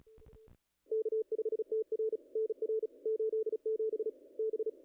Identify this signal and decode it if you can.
Morse code, heard as a keyed tone, about 36 wpm, 440 Hz. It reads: K5TR NR 87 B
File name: contests / ss-cw-k5tr-op-n5rz -0000